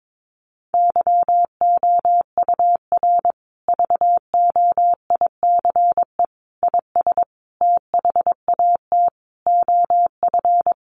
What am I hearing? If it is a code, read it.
Morse code, heard as a keyed tone, about 22 wpm, 700 Hz. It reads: YOUR VOICE IS THAT OF
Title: Sentences from Top 500 Words 22wpm